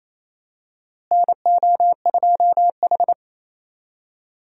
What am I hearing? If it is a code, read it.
Morse code, heard as a keyed tone, about 28 wpm, 700 Hz. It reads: NO2H
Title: USA Call Signs - 4 Characters - Repeated 2x - 28wpm